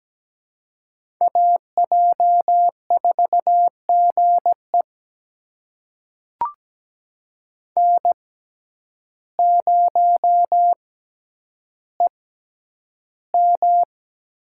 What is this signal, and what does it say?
Morse code, heard as a keyed tone, about 17 wpm, spaced as for 9 wpm, 700 Hz. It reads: AJ4GE N0EM
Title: USA Call Signs - 5 Characters - 6x Character Spacing - 17wpm